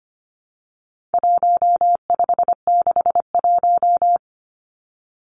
Morse code, heard as a keyed tone, about 25 wpm, 700 Hz. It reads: 1561
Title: Numbers - 4-Digit 25wpm